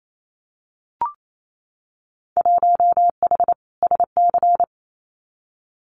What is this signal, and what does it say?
Morse code, heard as a keyed tone, about 28 wpm, 700 Hz. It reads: 1H SC
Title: Winter Field Day Exchanges 28wpm